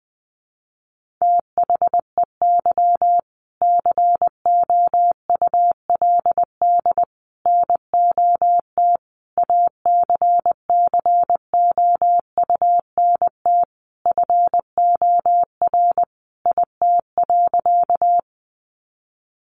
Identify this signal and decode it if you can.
Morse code, heard as a keyed tone, about 20 wpm, 700 Hz. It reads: THEY COULD NOT ACCOUNT FOR IT.
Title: Sentences from Top 900 Words 20wpm